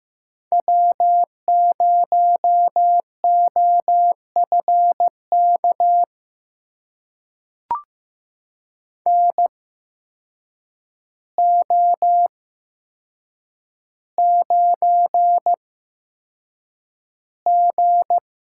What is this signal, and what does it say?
Morse code, heard as a keyed tone, about 15 wpm, spaced as for 7 wpm, 700 Hz. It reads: W0OFK NO9G
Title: USA Call Signs - 8x Character Spacing - 15wpm